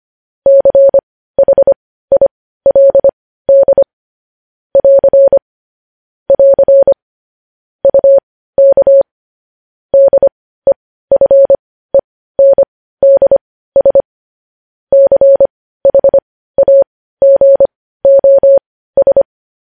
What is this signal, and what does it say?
Morse code, heard as a keyed tone, about 25 wpm, spaced as for 15 wpm, 550 Hz. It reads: CHILD <AR> <AR> UK DEFENDS CHAGOS
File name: podcast / News headlines-1x-25x15